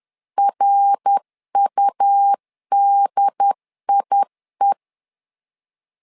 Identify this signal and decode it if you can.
Morse code, heard as a keyed tone, about 10 wpm, 800 Hz. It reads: RUDIE
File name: LEZIONI / 3 MWRO / 3 MWR 10wpm 1952